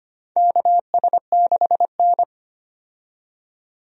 Morse code, heard as a keyed tone, about 25 wpm, 700 Hz. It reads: KS6N
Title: USA Call Signs - 4 Characters - 25wpm